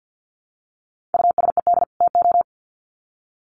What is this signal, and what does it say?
Morse code, heard as a keyed tone, about 50 wpm, 700 Hz. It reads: USED TO
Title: Sets of 2 Words 50wpm